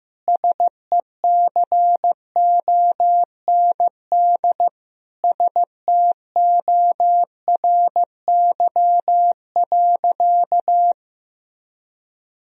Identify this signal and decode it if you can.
Morse code, heard as a keyed tone, about 15 wpm, 700 Hz. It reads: SECOND STORY.
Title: Sentences from Top 1000 Words - Review Part 3 - 15wpm